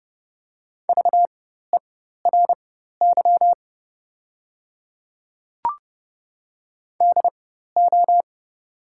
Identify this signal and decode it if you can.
Morse code, heard as a keyed tone, about 30 wpm, spaced as for 14 wpm, 700 Hz. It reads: VERY DO
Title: 4x Character Spacing - Top 200 Words - 30wpm